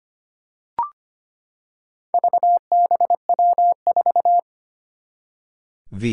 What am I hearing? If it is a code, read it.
Morse code, heard as a keyed tone, about 25 wpm, 700 Hz. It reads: VBW4
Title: Four Challenging Letter-Number Combinations 25wpm